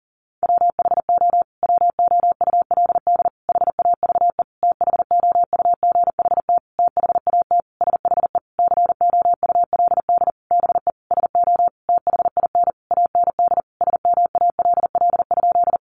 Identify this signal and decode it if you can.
Morse code, heard as a keyed tone, about 40 wpm, 700 Hz. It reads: WHO WOULD HAVE THOUGHT THAT SHE COULD BE SO THIN AND SMALL?
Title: Sentences from Top 1200 Words 40wpm